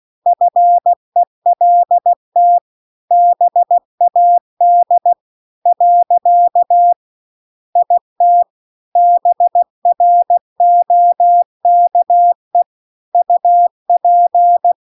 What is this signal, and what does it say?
Morse code, heard as a keyed tone, about 16 wpm, 700 Hz. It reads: FELT BAD . IT BROKE UP